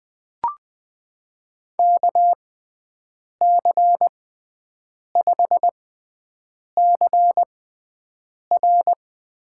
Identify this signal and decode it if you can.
Morse code, heard as a keyed tone, about 21 wpm, spaced as for 7 wpm, 700 Hz. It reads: KC5CR
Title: USA Call Signs - 6 Characters - 6x Character Spacing - 20wpm